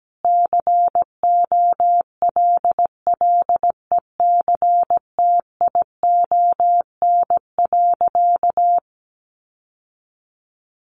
Morse code, heard as a keyed tone, about 17 wpm, 700 Hz. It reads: COLLECTION.
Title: Sentences from Top 1200 Words 17wpm